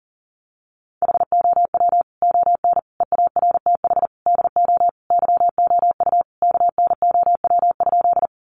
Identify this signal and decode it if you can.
Morse code, heard as a keyed tone, about 40 wpm, 700 Hz. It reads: HOW ON EARTH DO YOU KNOW?
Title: Sentences from Top 200 Words 40wpm